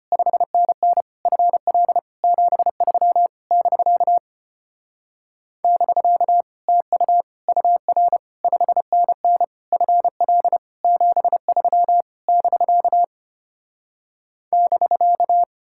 Morse code, heard as a keyed tone, about 33 wpm, 700 Hz. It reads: 5NN FL 73 <BK> <BK> TU UR 5NN FL 73 <BK> <BK>